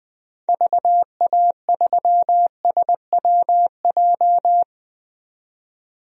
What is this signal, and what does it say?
Morse code, heard as a keyed tone, about 20 wpm, 700 Hz. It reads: VA3SWJ
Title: DXCC-100 Call Signs - 6 Characters - 20wpm